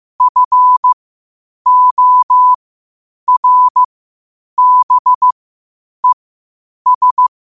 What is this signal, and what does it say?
Morse code, heard as a keyed tone, about 15 wpm, spaced as for 8 wpm, 1000 Hz. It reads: FORBES
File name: qotdfiles / 8wpm